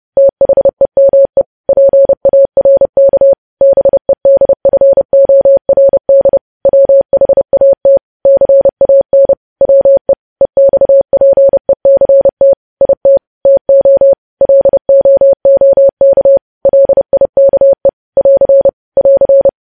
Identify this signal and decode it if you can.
Morse code, heard as a keyed tone, about 30 wpm, 550 Hz. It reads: THEME PARK BEDFORD WHAT CAN WE EXPECT IT TO LOOK LIKE <AR> <AR>
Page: news headlines in morse code at 30 wpm